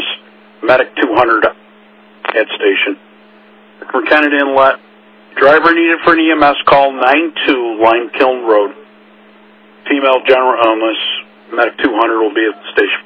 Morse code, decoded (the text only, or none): none